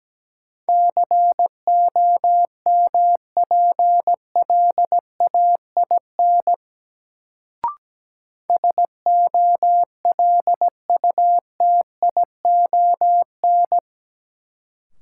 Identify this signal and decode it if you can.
Morse code, heard as a keyed tone, about 17 wpm, 700 Hz. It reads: COMPLAIN SOLUTION
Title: Sentences from Top 1300 Words - New Word Drill - 17wpm